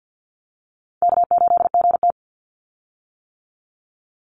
Morse code, heard as a keyed tone, about 50 wpm, 700 Hz. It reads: K8GT
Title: Call Signs - 50wpm